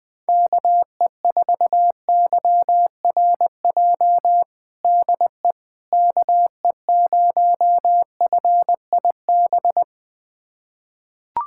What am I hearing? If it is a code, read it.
Morse code, heard as a keyed tone, about 20 wpm, 700 Hz. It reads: KE4YRJ DE KE0FIB
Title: USA Callsign DE Callsign - 6 Characters - 20wpm